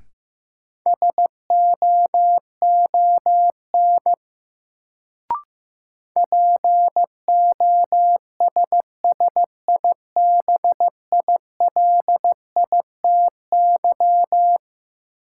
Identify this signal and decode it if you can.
Morse code, heard as a keyed tone, about 15 wpm, 700 Hz. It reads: SOON POSSIBILITY
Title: Sets of 3 Words - New Word Drill - 15wpm